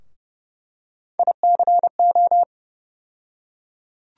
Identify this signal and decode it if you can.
Morse code, heard as a keyed tone, about 30 wpm, 700 Hz. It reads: ICO
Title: USA Call Sign Postfixes - 3 Letters 30wpm